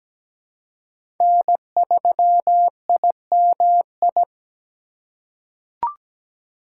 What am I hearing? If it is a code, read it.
Morse code, heard as a keyed tone, about 17 wpm, 700 Hz. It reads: N3IMI
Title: USA Call Signs - 4x Character Spacing - 17wpm